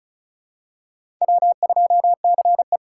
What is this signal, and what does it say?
Morse code, heard as a keyed tone, about 36 wpm, 700 Hz. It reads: W2CE
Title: USA Call Signs - 4 Characters - 4x Character Spacing - 35wpm